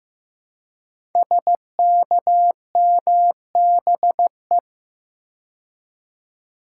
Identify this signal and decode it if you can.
Morse code, heard as a keyed tone, about 15 wpm, 700 Hz. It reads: SKMBE
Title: Code Groups - 5 Letters - 15wpm